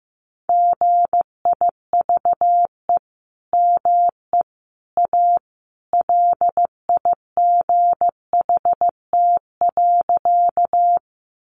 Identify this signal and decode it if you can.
Morse code, heard as a keyed tone, about 15 wpm, 700 Hz. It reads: GIVE ME A LIGHT.
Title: Sentences from Top 200 Words 15wpm